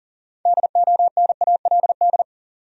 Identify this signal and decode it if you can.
Morse code, heard as a keyed tone, about 40 wpm, 700 Hz. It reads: DONALD
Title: USA Common Names - 40wpm